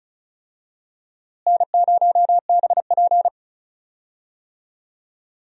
Morse code, heard as a keyed tone, about 36 wpm, 700 Hz. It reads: N0BP
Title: USA Call Signs - 4 Characters - Repeated 3x - 35wpm